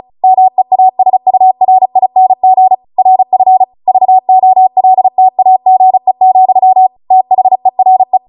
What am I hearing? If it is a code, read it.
Morse code, heard as a keyed tone, about 36 wpm, 750 Hz. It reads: MEASURING RF VOLTAGE, THERE